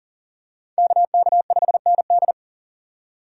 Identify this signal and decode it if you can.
Morse code, heard as a keyed tone, about 40 wpm, 700 Hz. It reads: KK5ND
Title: USA Call Signs - 5 Characters - 3x Character Spacing - 40wpm